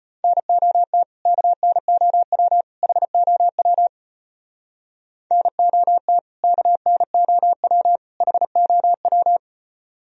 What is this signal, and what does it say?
Morse code, heard as a keyed tone, about 32 wpm, 700 Hz. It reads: NOT KNOW HOW NOT KNOW HOW